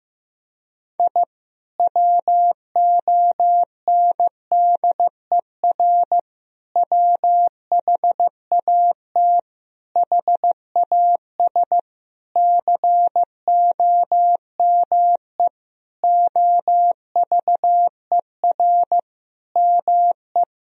Morse code, heard as a keyed tone, about 15 wpm, 700 Hz. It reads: I WONDER WHAT HAS COME OVER ME
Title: Sentences from Top 1000 Words - Review Part 4 - 15wpm